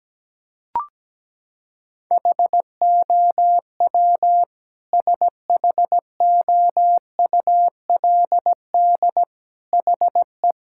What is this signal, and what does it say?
Morse code, heard as a keyed tone, about 17 wpm, 700 Hz. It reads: HOW SHOULD HE
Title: Sentences from Top 1000 Words - Review Part 6 - 17wpm